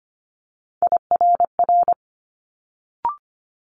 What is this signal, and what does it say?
Morse code, heard as a keyed tone, about 25 wpm, 700 Hz. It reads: IRR
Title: Multi-letter Phonemes Quick Fire 25wpm